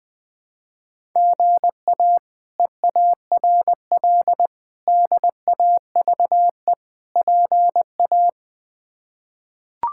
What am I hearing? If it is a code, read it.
Morse code, heard as a keyed tone, about 20 wpm, 700 Hz. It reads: GA EARL DAVE PA
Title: K1USN SST QSO Practice - 20wpm